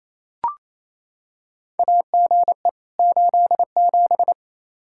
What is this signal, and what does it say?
Morse code, heard as a keyed tone, about 28 wpm, 700 Hz. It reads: AGE 87
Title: K1USN SST QSO Practice 28wpm